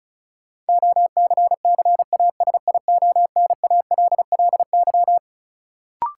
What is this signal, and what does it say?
Morse code, heard as a keyed tone, about 36 wpm, 700 Hz. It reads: OCCASIONALLY
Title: Sentences from Top 1300 Words - New Word Drill Alt - 35wpm